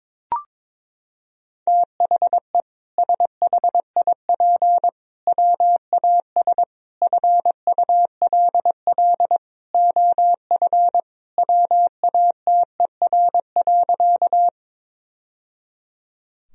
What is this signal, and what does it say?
Morse code, heard as a keyed tone, about 21 wpm, 700 Hz. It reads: THE SHIP WAS FULL OF WATER.
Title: Sentences from Top 1000 Words - Review Part 3 - 22wpm